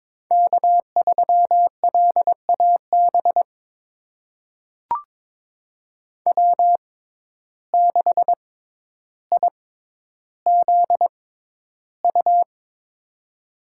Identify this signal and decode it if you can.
Morse code, heard as a keyed tone, about 22 wpm, spaced as for 11 wpm, 700 Hz. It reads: K3LAB W6IZU
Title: USA Call Signs - 5 Characters - 6x Character Spacing - 22wpm